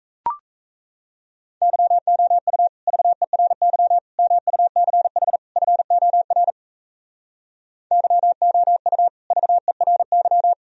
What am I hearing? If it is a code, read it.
Morse code, heard as a keyed tone, about 37 wpm, 700 Hz. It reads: YOU VERY MUCH FOR YOU VERY